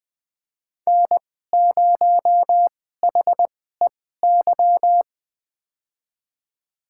Morse code, heard as a keyed tone, about 20 wpm, spaced as for 14 wpm, 700 Hz. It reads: N0HEY
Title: USA Call Signs - 5 Characters - 2x Character Spacing - 20wpm